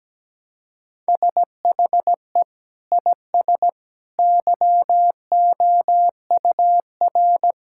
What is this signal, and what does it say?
Morse code, heard as a keyed tone, about 17 wpm, 700 Hz. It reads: SHE IS YOUR
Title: Sentences from Top 1600 Words 17wpm